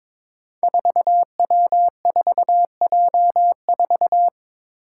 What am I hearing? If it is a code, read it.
Morse code, heard as a keyed tone, about 23 wpm, 700 Hz. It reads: 4W4J4
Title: Five Challenging Letter-Number Combinations 22wpm